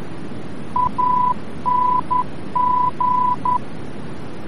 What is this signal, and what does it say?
Morse code, heard as a keyed tone, about 11 wpm, 1000 Hz, with noise. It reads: ANG
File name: france / mx / ANG-VOR-F